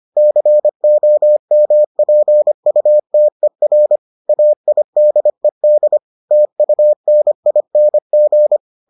Morse code, heard as a keyed tone, about 25 wpm, 600 Hz. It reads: COMPUTER AIDED TUNING